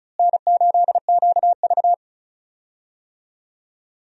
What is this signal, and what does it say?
Morse code, heard as a keyed tone, about 36 wpm, 700 Hz. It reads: N8QV
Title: USA Call Signs - 4 Characters - Repeated 2x - 35wpm